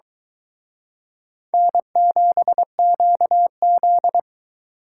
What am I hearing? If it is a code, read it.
Morse code, heard as a keyed tone, about 26 wpm, 700 Hz. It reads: N7QZ